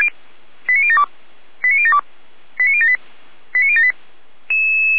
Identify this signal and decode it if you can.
Five-tone selective call (ZVEI), heard as quick runs of five tones, tones of about 70 ms. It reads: 89072 89072 89078 89078
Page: /455/SDR/ZVEI/89099/2025.08.13